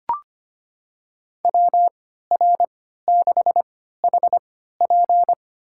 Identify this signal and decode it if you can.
Morse code, heard as a keyed tone, about 25 wpm, spaced as for 14 wpm, 700 Hz. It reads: WR6HP
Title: USA Call Signs - 5 Characters - 3x Character Spacing - 25wpm